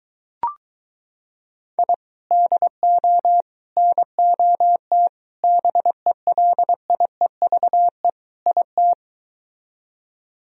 Morse code, heard as a keyed tone, about 23 wpm, 700 Hz. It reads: I DO NOT BELIEVE IT